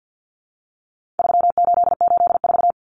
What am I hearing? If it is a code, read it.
Morse code, heard as a keyed tone, about 49 wpm, 700 Hz. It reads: 3884
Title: Common Cut Numbers - 4-Digit 50wpm